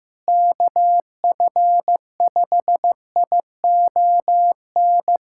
Morse code, heard as a keyed tone, about 15 wpm, 700 Hz. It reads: KF5ION
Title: USA Callsign DE Callsign - 6 Characters - 15wpm